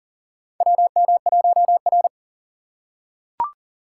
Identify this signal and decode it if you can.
Morse code, heard as a keyed tone, about 38 wpm, 700 Hz. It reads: WM1R
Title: USA Call Signs - 4 Characters - 6x Character Spacing - 40wpm